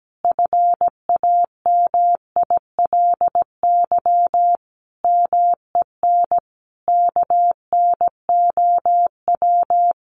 Morse code, heard as a keyed tone, about 17 wpm, 700 Hz. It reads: FAMILY MEN KNOW
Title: Sentences from Top 1400 Words 17wpm